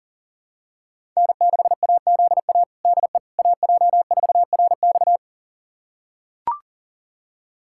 Morse code, heard as a keyed tone, about 40 wpm, 700 Hz. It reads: N6AZA DE AJ4RX
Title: USA Callsign DE Callsign - 5 Characters - 40wpm